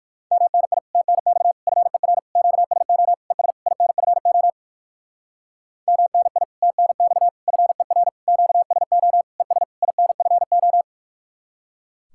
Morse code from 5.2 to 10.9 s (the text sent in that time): MNI TNX FER QSO ES INFO